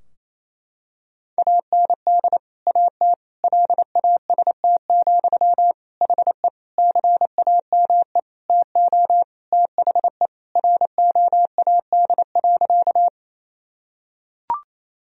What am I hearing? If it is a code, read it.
Morse code, heard as a keyed tone, about 28 wpm, 700 Hz. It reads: AND AT LAST, HE CAME TO THE ROAD.
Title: Sentences from Top 1000 Words - Review Part 2 - 28wpm